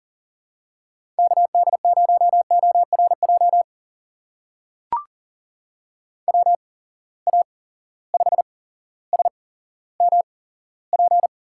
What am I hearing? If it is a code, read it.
Morse code, heard as a keyed tone, about 40 wpm, spaced as for 17 wpm, 700 Hz. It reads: KD0ORJ WA5SMP
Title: USA Call Signs - 6 Characters - 8x Character Spacing - 40wpm